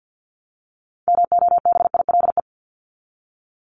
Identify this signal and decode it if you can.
Morse code, heard as a keyed tone, about 50 wpm, 700 Hz. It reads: MOBILE
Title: Top 1751-2000 Words 50wpm